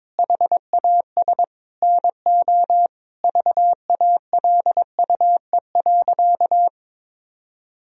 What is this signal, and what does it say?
Morse code, heard as a keyed tone, about 22 wpm, 700 Hz. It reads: HAS NO VALUE.
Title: Sentences from Top 1000 Words - Review Part 4 - 22wpm